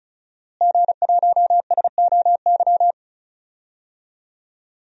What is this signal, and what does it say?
Morse code, heard as a keyed tone, about 34 wpm, 700 Hz. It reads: G1SOY
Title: DXCC-100 Call Signs - 5 Characters - Repeated 2x - 35wpm